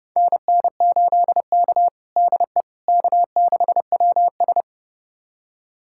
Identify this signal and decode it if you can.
Morse code, heard as a keyed tone, about 30 wpm, 700 Hz. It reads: NN8K DE K6WS
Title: USA Callsign DE Callsign - 4 Characters - 30wpm